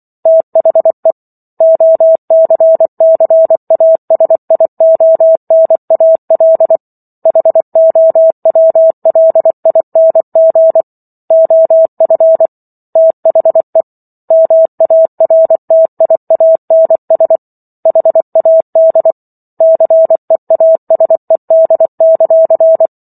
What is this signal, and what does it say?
Morse code, heard as a keyed tone, about 24 wpm, 650 Hz. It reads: THE OCCASIONAL HOWLING OF THE MARTIANS HAD CEASED;